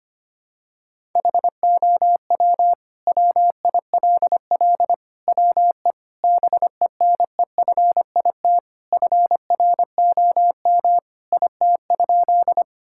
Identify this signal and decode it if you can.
Morse code, heard as a keyed tone, about 25 wpm, 700 Hz. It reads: HOW WILL WE BENEFIT FROM IT?
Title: Sentences from Top 1000 Words - Review Part 6 - 25wpm